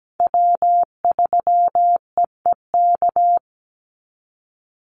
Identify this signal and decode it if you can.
Morse code, heard as a keyed tone, about 17 wpm, 700 Hz. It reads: W3EEK
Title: Call Signs Repeated 3x - 17wpm